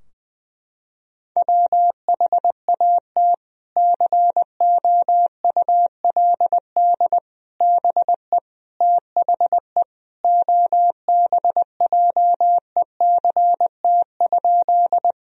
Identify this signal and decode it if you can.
Morse code, heard as a keyed tone, about 20 wpm, 700 Hz. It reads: WHAT COULD BE THE OBJECT?
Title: Sentences from Top 1000 Words - Review Part 3 - 20wpm